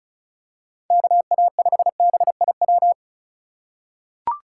Morse code, heard as a keyed tone, about 36 wpm, 700 Hz. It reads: KA5BIW
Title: USA Call Signs - 6 Characters - 2x Character Spacing - 35wpm